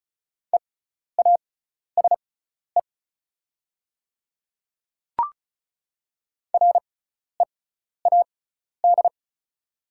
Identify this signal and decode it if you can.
Morse code, heard as a keyed tone, about 36 wpm, spaced as for 12 wpm, 700 Hz. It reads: EASE READ
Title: 6x Character Spacing - Top 300 Words - 35wpm